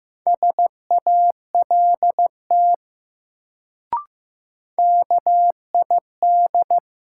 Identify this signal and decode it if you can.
Morse code, heard as a keyed tone, about 15 wpm, 700 Hz. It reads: SALT KID
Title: Sets of 5 Words - New Word Drill - 15wpm